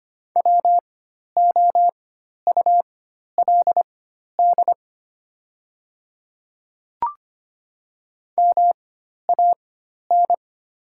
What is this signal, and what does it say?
Morse code, heard as a keyed tone, about 25 wpm, spaced as for 12 wpm, 700 Hz. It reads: WOULD MAN
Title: 4x Character Spacing - Top 300 Words - 25wpm